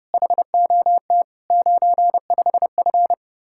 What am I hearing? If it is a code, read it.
Morse code, heard as a keyed tone, about 30 wpm, 700 Hz. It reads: HOT 95F